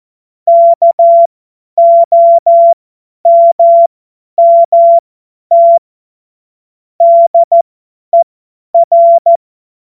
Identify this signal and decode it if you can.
Morse code, heard as a keyed tone, about 14 wpm, spaced as for 10 wpm, 680 Hz. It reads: KOMMT DER
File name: Morsekurs / Witze / Witz 20 Februar 2024 50 F70